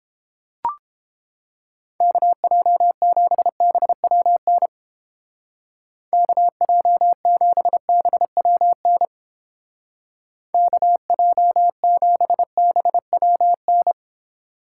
Morse code, heard as a keyed tone, about 28 wpm, 700 Hz. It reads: KJ7BWN KJ7BWN KJ7BWN